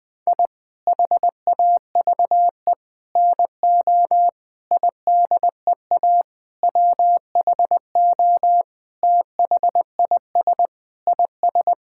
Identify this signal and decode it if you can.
Morse code, heard as a keyed tone, about 20 wpm, 700 Hz. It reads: I HAVE NO IDEA WHO THIS IS
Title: Sentences with 9-Word Phrases - 20wpm